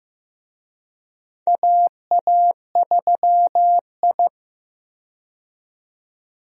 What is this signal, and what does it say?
Morse code, heard as a keyed tone, about 15 wpm, 700 Hz. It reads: AA3I
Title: USA Call Signs - 4 Characters - 15wpm